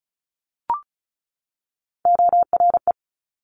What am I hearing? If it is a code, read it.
Morse code, heard as a keyed tone, about 36 wpm, 700 Hz. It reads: ORE
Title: Multi-letter Phonemes 35wpm